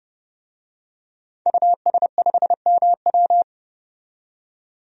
Morse code, heard as a keyed tone, about 30 wpm, 700 Hz. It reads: US5MW
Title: DXCC-100 Call Signs - 5 Characters - 30wpm